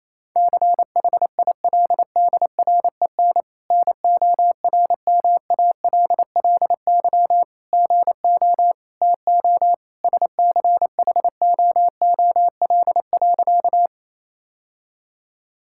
Morse code, heard as a keyed tone, about 27 wpm, 700 Hz. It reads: CHILDREN NORMALLY GO TO SCHOOL.
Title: Sentences from Top 1000 Words - Review Part 6 - 28wpm